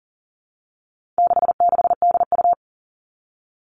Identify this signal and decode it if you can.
Morse code, heard as a keyed tone, about 40 wpm, 700 Hz. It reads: 66DU
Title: All Cut Numbers - 4-Digit 40wpm